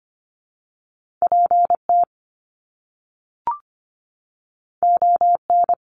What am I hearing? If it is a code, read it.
Morse code, heard as a keyed tone, about 25 wpm, 700 Hz. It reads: PT ON
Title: Top Two Letter Combinations in Words 25wpm